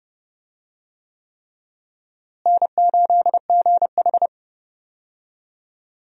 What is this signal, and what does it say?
Morse code, heard as a keyed tone, about 30 wpm, 700 Hz. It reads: N8GH
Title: USA Call Signs - 4 Characters - Repeated 3x - 30wpm